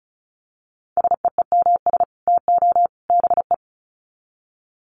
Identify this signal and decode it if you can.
Morse code, heard as a keyed tone, about 34 wpm, 700 Hz. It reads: SEEMS TO BE
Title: Sets of 3 Words 35wpm